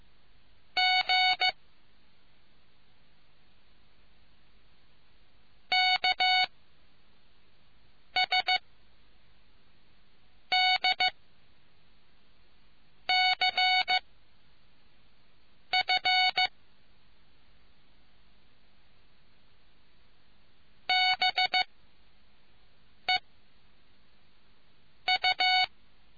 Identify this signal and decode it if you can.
Morse code, heard as a keyed tone, about 15 wpm, 740 Hz. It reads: G K S D C F B E U